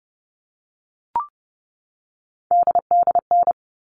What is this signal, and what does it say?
Morse code, heard as a keyed tone, about 30 wpm, 700 Hz. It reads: DDN
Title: All Cut Numbers - 3-Digit 30wpm